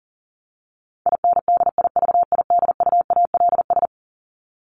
Morse code, heard as a keyed tone, about 40 wpm, 700 Hz. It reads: INDIVIDUALS
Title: ES:S Suffix - Top 100 Words with Base Words 40wpm